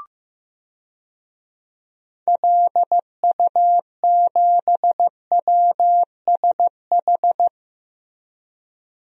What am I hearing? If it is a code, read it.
Morse code, heard as a keyed tone, about 16 wpm, 700 Hz. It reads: LU7WSH
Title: DXCC-100 Call Signs - 6 Characters - Repeated 3x - 15wpm